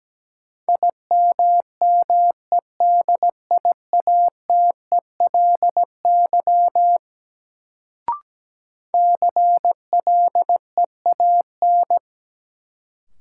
Morse code, heard as a keyed tone, about 17 wpm, 700 Hz. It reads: IMMEDIATELY CLEAN
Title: Sentences from Top 1100 Words - New Word Drill - 17wpm